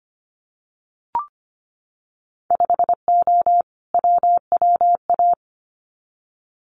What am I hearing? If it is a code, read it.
Morse code, heard as a keyed tone, about 25 wpm, 690 Hz. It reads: 5O WWA
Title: Winter Field Day Exchanges Encore 25wpm